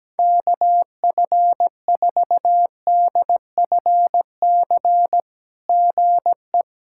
Morse code, heard as a keyed tone, about 18 wpm, 700 Hz. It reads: KF4DFC GE
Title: SOTA QSO Practice 17wpm